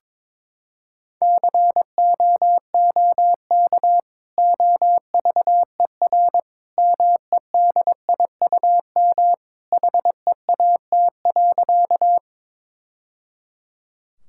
Morse code, heard as a keyed tone, about 21 wpm, 700 Hz. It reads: COOK OVER MEDIUM HEAT.
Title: Sentences from Top 1000 Words - Review Part 6 - 22wpm